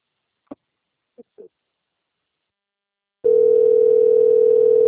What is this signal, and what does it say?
Ringback tone (North America), 440 + 480 Hz.
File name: rg-1600-unknown- - - .23860